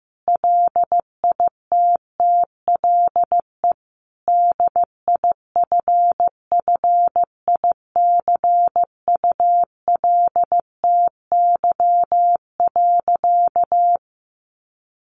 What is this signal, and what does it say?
Morse code, heard as a keyed tone, about 15 wpm, 700 Hz. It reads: LITTLE DIFFICULTY.
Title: Sentences from Top 1500 Words 15wpm